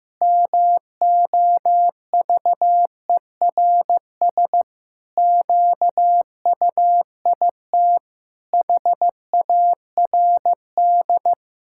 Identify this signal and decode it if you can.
Morse code, heard as a keyed tone, about 15 wpm, 700 Hz. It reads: MOVERS QUIT HARD
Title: Pangram Challenge v2 - 15wpm